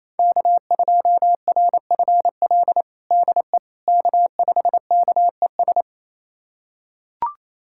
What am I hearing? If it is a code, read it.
Morse code, heard as a keyed tone, about 27 wpm, 700 Hz. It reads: K2RFL DE K5KES
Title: USA Callsign DE Callsign - 5 Characters - 28wpm